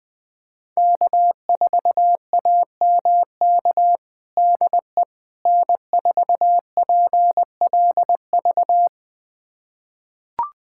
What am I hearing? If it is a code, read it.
Morse code, heard as a keyed tone, about 21 wpm, 700 Hz. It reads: K4AMK DE N4PLV
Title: USA Callsign DE Callsign - 5 Characters - 20wpm